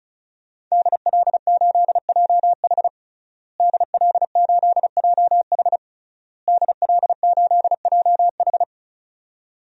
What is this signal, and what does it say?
Morse code, heard as a keyed tone, about 36 wpm, 700 Hz. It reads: DL8JH DL8JH DL8JH
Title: DXCC-100 Call Signs - 5 Characters - Repeated 3x - 35wpm